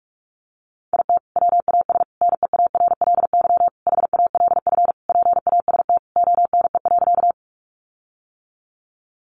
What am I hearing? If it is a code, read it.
Morse code, heard as a keyed tone, about 44 wpm, 700 Hz. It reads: IT WAS NEARLY HALF PAST ONE.
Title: Sentences from Top 1300 Words 45wpm